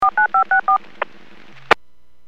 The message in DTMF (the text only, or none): 49564